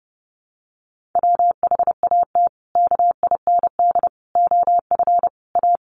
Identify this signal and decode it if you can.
Morse code, heard as a keyed tone, about 29 wpm, 700 Hz. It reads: WHAT KIND OF A
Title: Sentences from Top 400 Words 30wpm